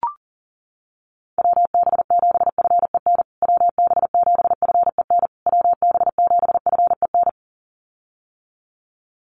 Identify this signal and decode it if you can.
Morse code, heard as a keyed tone, about 40 wpm, 700 Hz. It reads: WB7FEN WB7FEN WB7FEN
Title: USA Call Signs Repeated 3x 40wpm